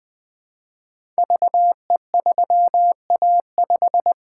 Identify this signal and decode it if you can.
Morse code, heard as a keyed tone, about 20 wpm, 700 Hz. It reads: VE3A5
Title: Code Groups - 5 Letter:Numbers - 20wpm